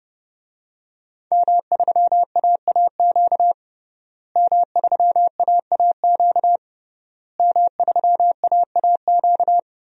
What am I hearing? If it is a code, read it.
Morse code, heard as a keyed tone, about 30 wpm, 700 Hz. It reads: M3AAQ M3AAQ M3AAQ
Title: DXCC-100 Call Signs - 5 Characters - Repeated 3x - 30wpm